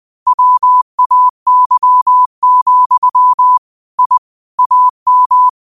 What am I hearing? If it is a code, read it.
Morse code, heard as a keyed tone, about 20 wpm, 1000 Hz. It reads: WAY, I AM
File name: qotdfiles / 20wpm